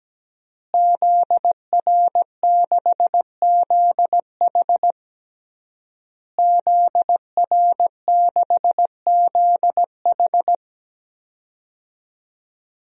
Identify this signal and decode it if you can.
Morse code, heard as a keyed tone, about 17 wpm, 700 Hz. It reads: ZR6ZH ZR6ZH
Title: DXCC-100 Call Signs - 5 Characters - Repeated 3x - 17wpm